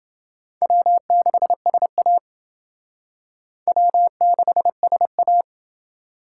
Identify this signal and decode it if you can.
Morse code, heard as a keyed tone, about 28 wpm, 700 Hz. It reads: W6SA W6SA